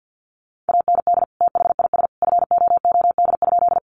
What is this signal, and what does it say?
Morse code, heard as a keyed tone, about 50 wpm, 700 Hz. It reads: AND THIS FOOD?
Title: Sentences from Top 300 Words 50wpm